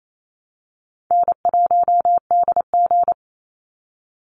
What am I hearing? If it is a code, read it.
Morse code, heard as a keyed tone, about 28 wpm, 700 Hz. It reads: N1DG
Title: Easy Call Signs - 28wpm